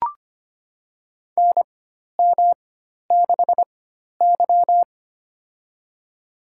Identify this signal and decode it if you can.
Morse code, heard as a keyed tone, about 25 wpm, spaced as for 12 wpm, 700 Hz. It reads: NM6Y
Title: USA Call Signs - 4 Characters - 4x Character Spacing - 25wpm